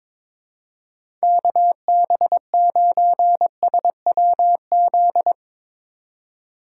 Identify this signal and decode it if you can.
Morse code, heard as a keyed tone, about 24 wpm, 700 Hz. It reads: KB9SWZ
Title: Speed Racing - USA Call Signs 22wpm